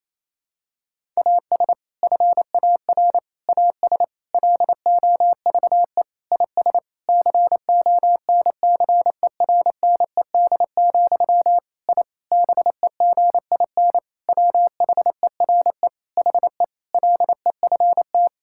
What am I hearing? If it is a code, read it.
Morse code, heard as a keyed tone, about 27 wpm, 700 Hz. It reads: AS FAR AS LOVE IS CONCERNED, I BEGIN WHERE HE LEFT
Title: Sentences from Top 1000 Words - Review Part 6 - 28wpm